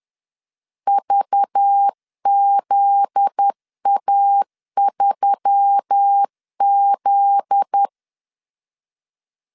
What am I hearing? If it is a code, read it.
Morse code, heard as a keyed tone, about 10 wpm, 800 Hz. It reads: VZA3Z
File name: LEZIONI / 7 GF9Z / 7 GF9Z 10wpm 2611